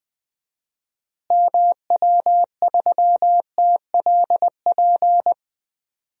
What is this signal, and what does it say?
Morse code, heard as a keyed tone, about 20 wpm, 700 Hz. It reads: MW3TLP
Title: DXCC-100 Call Signs - 6 Characters - Repeated 2x - 20wpm